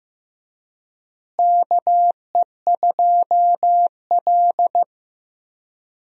Morse code, heard as a keyed tone, about 15 wpm, 700 Hz. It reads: KE2L
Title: USA Call Signs - 4 Characters - Repeated 3x - 15wpm